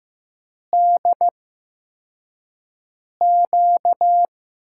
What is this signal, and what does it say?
Morse code, heard as a keyed tone, about 15 wpm, 700 Hz. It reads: D Q
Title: USA Call Signs - 8x Character Spacing - 15wpm